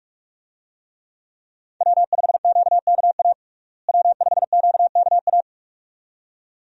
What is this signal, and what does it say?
Morse code, heard as a keyed tone, about 46 wpm, 700 Hz. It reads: W5QKA W5QKA
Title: USA Call Signs - 5 Characters - Repeated 2x - 45wpm